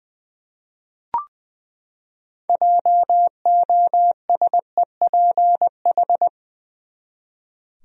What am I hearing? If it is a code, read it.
Morse code, heard as a keyed tone, about 20 wpm, 700 Hz. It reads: JOSEPH
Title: USA Common Names - 20wpm